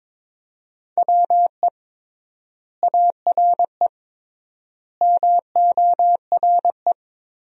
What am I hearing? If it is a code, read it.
Morse code, heard as a keyed tone, about 22 wpm, 700 Hz. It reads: WE ARE MORE
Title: 3x Word-Spacing - Sentences from Top 500 Words - 22wpm